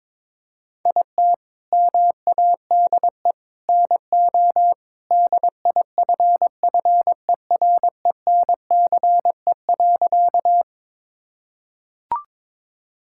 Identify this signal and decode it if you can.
Morse code, heard as a keyed tone, about 22 wpm, 700 Hz. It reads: IT MADE NO DIFFERENCE.
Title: Sentences from Top 1000 Words - Review Part 5 - 22wpm